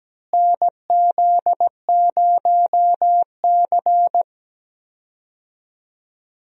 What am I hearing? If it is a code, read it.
Morse code, heard as a keyed tone, about 17 wpm, 700 Hz. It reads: NZ0C
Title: USA Call Signs - 4 Characters - Repeated 2x - 17wpm